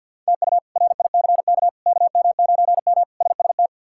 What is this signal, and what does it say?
Morse code, heard as a keyed tone, about 49 wpm, 700 Hz. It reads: TU RICK KM9M SST
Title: K1USN SST QSO Practice - 50wpm